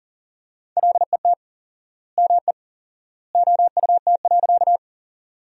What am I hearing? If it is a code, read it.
Morse code, heard as a keyed tone, about 40 wpm, 700 Hz. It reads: LET ME OUT.
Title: 4x Word-Spacing - Sentences from Top 300 Words - 40wpm